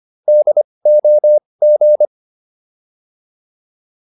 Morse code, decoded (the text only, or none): DOG